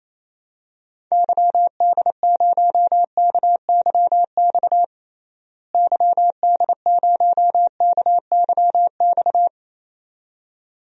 Morse code, heard as a keyed tone, about 28 wpm, 700 Hz. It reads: YD0KYX YD0KYX
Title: DXCC-100 Call Signs - 6 Characters - Repeated 2x - 28wpm